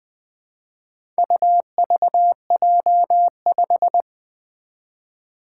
Morse code, heard as a keyed tone, about 20 wpm, 700 Hz. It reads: UVJ5
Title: Four Challenging Letter-Number Combinations 20wpm